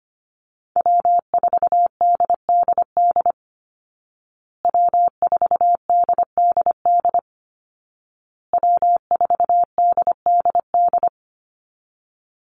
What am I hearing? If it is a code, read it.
Morse code, heard as a keyed tone, about 25 wpm, 700 Hz. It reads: W4DDD W4DDD W4DDD
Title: Call Signs Repeated 2x - 25wpm